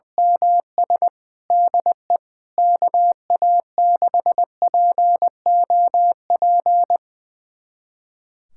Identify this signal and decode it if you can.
Morse code, heard as a keyed tone, about 20 wpm, 700 Hz. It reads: MS DE KA6POP
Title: USA Callsign DE Callsign 20wpm